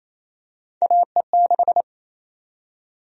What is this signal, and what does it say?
Morse code, heard as a keyed tone, about 28 wpm, 700 Hz. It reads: AE6
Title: USA Call Sign Prefixes - Repeated 2x - 28wpm